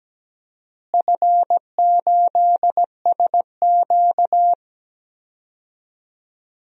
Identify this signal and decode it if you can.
Morse code, heard as a keyed tone, about 17 wpm, 700 Hz. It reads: F8SQ
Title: DXCC-100 Call Signs - 4 Characters - Repeated 2x - 17wpm